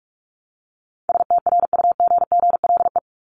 Morse code, heard as a keyed tone, about 44 wpm, 700 Hz. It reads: STRUGGLE
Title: Top 1751-2000 Words 45wpm